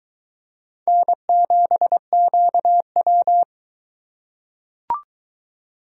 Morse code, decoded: N7QW

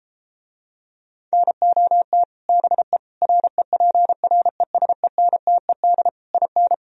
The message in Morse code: NOT BE REPRESENTED IN

